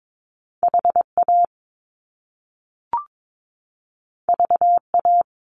Morse code, heard as a keyed tone, about 22 wpm, 700 Hz. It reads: HA VA
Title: Top Two Letter Combinations in Words 22wpm